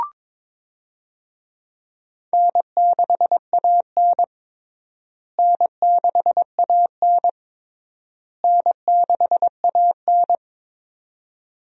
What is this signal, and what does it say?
Morse code, heard as a keyed tone, about 22 wpm, 700 Hz. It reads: N6AN N6AN N6AN
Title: USA Call Signs - 4 Characters - Repeated 3x - 22wpm